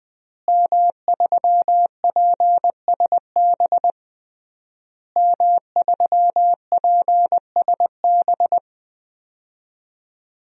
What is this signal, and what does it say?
Morse code, heard as a keyed tone, about 20 wpm, 700 Hz. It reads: M3PSB M3PSB
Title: DXCC-100 Call Signs - 5 Characters - Repeated 3x - 20wpm